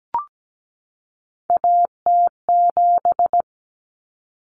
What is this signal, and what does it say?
Morse code, heard as a keyed tone, about 17 wpm, 700 Hz. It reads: AT7
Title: Common Cut Numbers - 3-Digit 17wpm